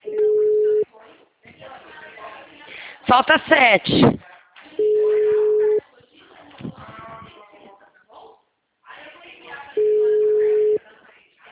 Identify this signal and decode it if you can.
Ringback tone (Europe, 425 Hz), with speech.